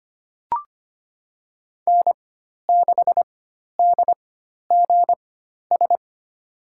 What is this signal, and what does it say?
Morse code, heard as a keyed tone, about 25 wpm, spaced as for 12 wpm, 700 Hz. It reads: N6DGS
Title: USA Call Signs - 5 Characters - 4x Character Spacing - 25wpm